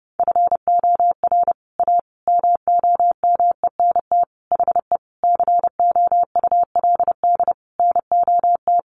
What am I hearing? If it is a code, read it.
Morse code, heard as a keyed tone, about 30 wpm, 700 Hz. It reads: FOR A MOMENT HE COULD NOT
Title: Sentences from Top 1100 Words 30wpm